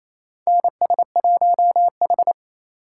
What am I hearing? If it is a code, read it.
Morse code, heard as a keyed tone, about 28 wpm, 700 Hz. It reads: NS1H
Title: USA Call Signs - 3x Character Spacing - 28wpm